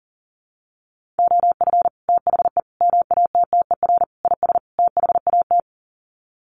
Morse code, heard as a keyed tone, about 40 wpm, 700 Hz. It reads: OF THE MATTER IS THAT
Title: Sets of 5 Words Encore 40wpm